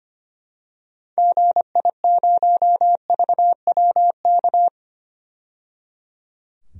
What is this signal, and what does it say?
Morse code, heard as a keyed tone, about 25 wpm, 700 Hz. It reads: GI0VWK
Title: DXCC-100 Call Signs - 6 Characters - 25wpm